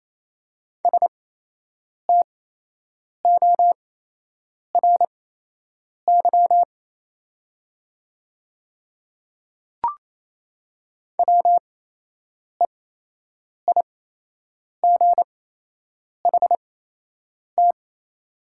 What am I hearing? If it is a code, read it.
Morse code, heard as a keyed tone, about 28 wpm, spaced as for 8 wpm, 700 Hz. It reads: STORY WEIGHT
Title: 8x Character Spacing - Top 500 Words - 28wpm